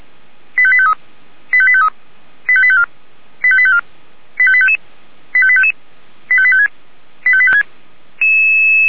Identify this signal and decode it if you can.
Five-tone selective call (ZVEI), heard as quick runs of five tones, tones of about 70 ms. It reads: 85853 85853 85854 85854 85855 85855 85856 85856
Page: /464/SDR/ZVEI/85738/2022.05.06/